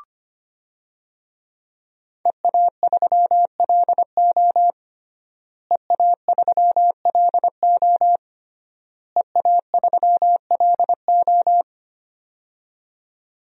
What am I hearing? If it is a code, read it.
Morse code, heard as a keyed tone, about 25 wpm, 700 Hz. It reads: EA3LO EA3LO EA3LO
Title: DXCC-100 Call Signs - 5 Characters - Repeated 3x - 25wpm